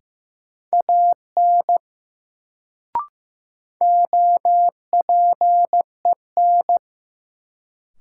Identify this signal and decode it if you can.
Morse code, heard as a keyed tone, about 15 wpm, 700 Hz. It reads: AN OPEN
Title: Sentences from Top 300 Words - New Word Drill - 15wpm